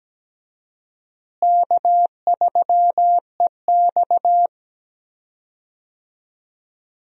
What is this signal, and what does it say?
Morse code, heard as a keyed tone, about 17 wpm, 700 Hz. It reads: K3EX
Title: USA Call Signs - 4 Characters - Repeated 3x - 17wpm